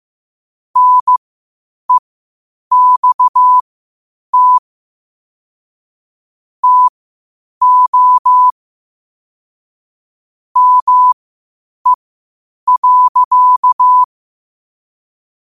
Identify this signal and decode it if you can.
Morse code, heard as a keyed tone, about 15 wpm, spaced as for 8 wpm, 1000 Hz. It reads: NEXT TO ME.
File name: qotdfiles / 8wpm